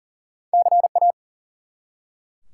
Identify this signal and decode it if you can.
Morse code, heard as a keyed tone, about 40 wpm, 700 Hz. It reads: CA
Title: US States and Canadian Provinces 40wpm